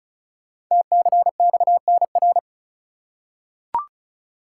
Morse code, heard as a keyed tone, about 36 wpm, 700 Hz. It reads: TCXNR
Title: Code Groups - 5 Letters - 35wpm